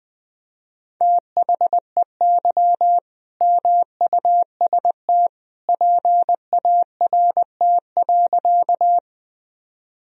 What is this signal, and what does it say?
Morse code, heard as a keyed tone, about 21 wpm, 700 Hz. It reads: THEY MUST PART.
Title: Sentences from Top 1000 Words - Review Part 1 - 20wpm